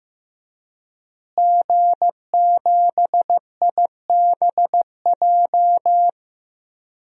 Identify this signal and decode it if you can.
Morse code, heard as a keyed tone, about 15 wpm, 700 Hz. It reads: G7IBJ
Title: DXCC-100 Call Signs - 5 Characters - Repeated 3x - 15wpm